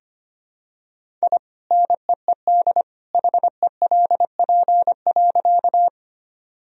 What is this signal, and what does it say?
Morse code, heard as a keyed tone, about 25 wpm, 700 Hz. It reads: I NEED HELP.
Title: Sentences from Top 1000 Words - Review Part 1 - 25wpm